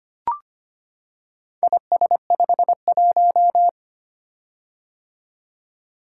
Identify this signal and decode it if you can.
Morse code, heard as a keyed tone, about 25 wpm, 700 Hz. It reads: IS51
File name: practice / 25wpm / Four Letter-Number Combinations - 25wpm